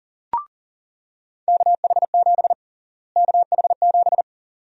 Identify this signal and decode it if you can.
Morse code, heard as a keyed tone, about 40 wpm, 700 Hz. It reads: KH7 KH7
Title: USA Call Sign Prefixes - Repeated 2x - 40wpm